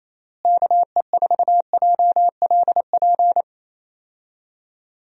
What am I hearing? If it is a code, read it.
Morse code, heard as a keyed tone, about 27 wpm, 700 Hz. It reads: KE4JLP
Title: USA Call Signs - 6 Characters - Repeated 3x - 28wpm